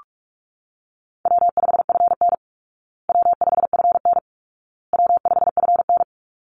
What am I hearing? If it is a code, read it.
Morse code, heard as a keyed tone, about 44 wpm, 700 Hz. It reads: W5FN W5FN W5FN
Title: Call Signs Repeated 2x - 45wpm